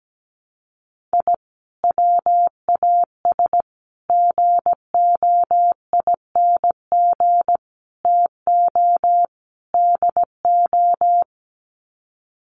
Morse code, heard as a keyed tone, about 17 wpm, 700 Hz. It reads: I WAS GOING TO DO
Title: Sets of 5 Words Encore2 17wpm